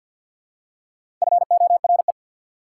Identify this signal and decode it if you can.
Morse code, heard as a keyed tone, about 53 wpm, 690 Hz. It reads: FORE
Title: Prefixes and Suffixes 50wpm